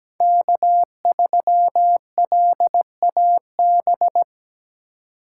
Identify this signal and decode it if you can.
Morse code, heard as a keyed tone, about 17 wpm, 700 Hz. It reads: K3LAB
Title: USA Call Signs - 6x Character Spacing - 17wpm